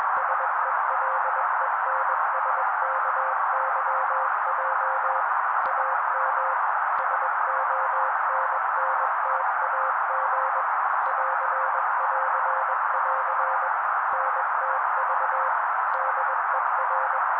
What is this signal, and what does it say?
Morse code, heard as a keyed tone, about 21 wpm, 530 Hz, with noise. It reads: SELENSKYJ AM SONNTAG <AR><AR><AR> NTV DER